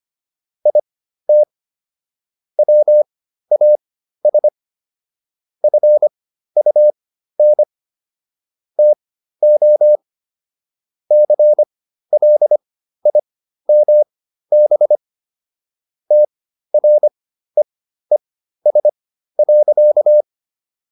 Morse code, heard as a keyed tone, about 25 wpm, spaced as for 13 wpm, 600 Hz. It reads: IT WAS FUN TO CLIMB TREES.